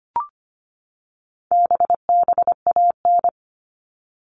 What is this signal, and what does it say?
Morse code, heard as a keyed tone, about 25 wpm, 700 Hz. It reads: BBAN
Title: All Cut Numbers - 4-Digit 25wpm